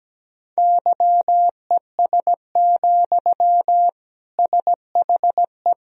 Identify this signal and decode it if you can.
Morse code, heard as a keyed tone, about 17 wpm, 700 Hz. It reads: YES, SHE
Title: Sentences from Top 1000 Words - Review Part 2 - 17wpm